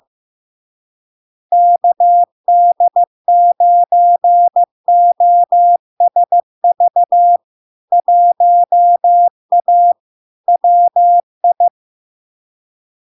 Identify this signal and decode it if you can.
Morse code, heard as a keyed tone, about 15 wpm, 700 Hz. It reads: KD9OSV 1A WI
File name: LARK-FD-2022-Log-in-CW-15wpm-700-Hz